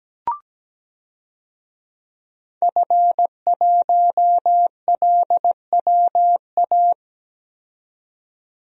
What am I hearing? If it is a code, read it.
Morse code, heard as a keyed tone, about 17 wpm, 700 Hz. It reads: F1LWA
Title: DXCC-100 Call Signs - 5 Characters - 17wpm